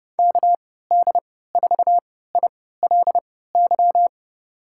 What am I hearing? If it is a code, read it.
Morse code, heard as a keyed tone, about 32 wpm, spaced as for 17 wpm, 700 Hz. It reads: KD4ILY
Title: USA Call Signs - 6 Characters - 3x Character Spacing - 30wpm